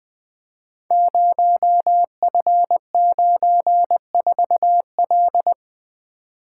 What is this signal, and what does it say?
Morse code, heard as a keyed tone, about 20 wpm, 700 Hz. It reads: 0F94L
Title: Code Groups - 5 Letter:Numbers - 20wpm